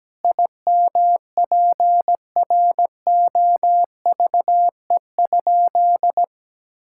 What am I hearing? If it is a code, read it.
Morse code, heard as a keyed tone, about 17 wpm, 700 Hz. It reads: IMPROVE?
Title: Sentences from Top 1000 Words - Review Part 5 - 17wpm